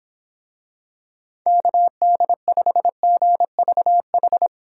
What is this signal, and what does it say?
Morse code, heard as a keyed tone, about 26 wpm, 700 Hz. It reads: KD5GVH